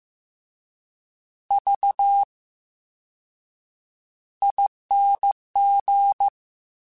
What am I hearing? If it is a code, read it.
Morse code, heard as a keyed tone, about 15 wpm, spaced as for 7 wpm, 800 Hz. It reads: VING
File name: Special INGs / Special INGs-mp3s / Slow ing @13